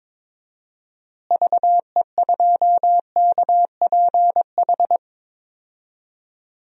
Morse code, heard as a keyed tone, about 21 wpm, 700 Hz. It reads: VE2KPH
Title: DXCC-100 Call Signs - 6 Characters - Repeated 2x - 22wpm